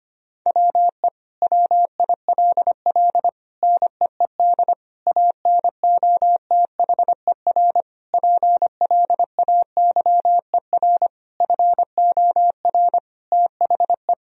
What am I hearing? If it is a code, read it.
Morse code, heard as a keyed tone, about 25 wpm, 700 Hz. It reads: WE WILL NEED ANOTHER PLAYER FOR THE
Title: Sentences from Top 1000 Words - Review Part 4 - 25wpm